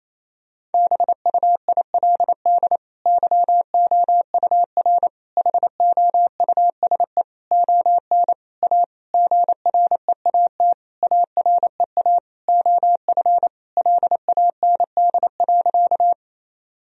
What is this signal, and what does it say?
Morse code, heard as a keyed tone, about 28 wpm, 700 Hz. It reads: BUILD YOUR HOUSE ON A GREAT AREA OF LAND.
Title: Sentences from Top 1000 Words - Review Part 2 - 28wpm